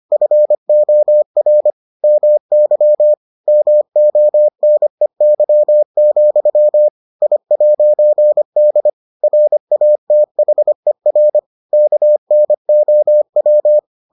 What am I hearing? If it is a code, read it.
Morse code, heard as a keyed tone, about 25 wpm, 600 Hz. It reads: FOR MY MONEY, I'D RATHER KNOW